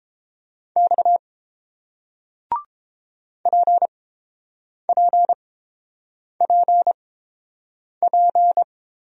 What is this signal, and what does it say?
Morse code, heard as a keyed tone, about 29 wpm, spaced as for 7 wpm, 690 Hz. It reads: X PPPP